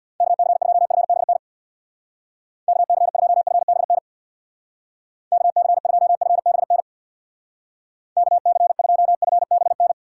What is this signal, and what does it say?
Morse code, heard as a keyed tone, about 62 wpm, 700 Hz. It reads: KC2FBN KC2FBN KC2FBN KC2FBN
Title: Speed Racing - USA 2x3 Call Signs 50wpm